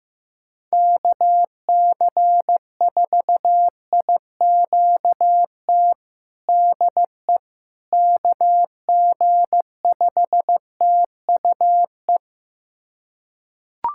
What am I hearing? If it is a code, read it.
Morse code, heard as a keyed tone, about 16 wpm, 700 Hz. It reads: KC4IQT DE KG5TUE
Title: USA Callsign DE Callsign 15wpm